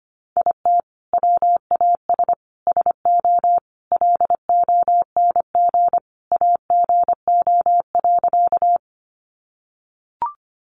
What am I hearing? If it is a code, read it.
Morse code, heard as a keyed tone, about 25 wpm, 700 Hz. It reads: IT WAS SO LONG AGO.
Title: Sentences from Top 500 Words 25wpm